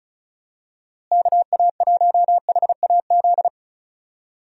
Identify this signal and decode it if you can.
Morse code, heard as a keyed tone, about 36 wpm, 700 Hz. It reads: KA1HAZ
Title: USA Call Signs - 6 Characters - Repeated 2x - 35wpm